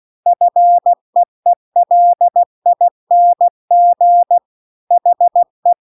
Morse code, heard as a keyed tone, about 16 wpm, 700 Hz. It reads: FEELING HE